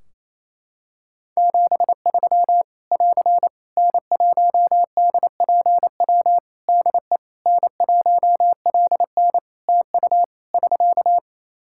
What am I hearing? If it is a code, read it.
Morse code, heard as a keyed tone, about 28 wpm, 700 Hz. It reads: 73 <AR> N1DPW DE N1LN TU <SK>
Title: Practice QSO - Exchange 4 - Ending - 28wpm